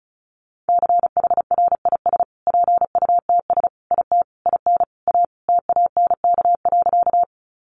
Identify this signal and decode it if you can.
Morse code, heard as a keyed tone, about 34 wpm, 700 Hz. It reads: CHRIS PUTS IT IN A TANK.
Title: Additional Easy Phrases 35wpm